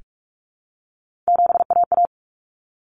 Morse code, heard as a keyed tone, about 44 wpm, 700 Hz. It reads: 7AA
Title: Common Cut Numbers - 3-Digit 45wpm